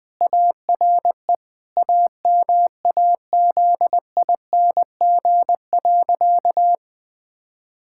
Morse code, heard as a keyed tone, about 20 wpm, 700 Hz. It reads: ARE AMAZING.
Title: Sentences with 3-Word Phrases - Spoken Beginning - 20wpm